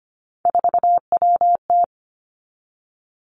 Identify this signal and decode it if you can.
Morse code, heard as a keyed tone, about 25 wpm, 700 Hz. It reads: 4WT
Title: Three Challenging Letter-Number Combinations 25wpm